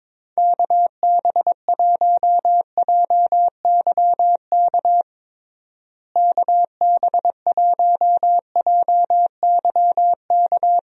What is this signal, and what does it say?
Morse code, heard as a keyed tone, about 22 wpm, 700 Hz. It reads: KB1JYK KB1JYK
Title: USA Call Signs - 6 Characters - Repeated 2x - 22wpm